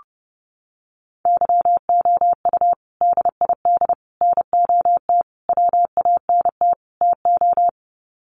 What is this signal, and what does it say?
Morse code, heard as a keyed tone, about 30 wpm, 700 Hz. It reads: YOU DID NOT WANT TO
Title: Sets of 5 Words Encore2 30wpm